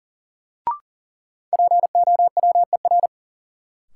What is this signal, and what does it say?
Morse code, heard as a keyed tone, about 40 wpm, 700 Hz. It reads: POWER
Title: Sets of 4 Words - New Word Drill - 40wpm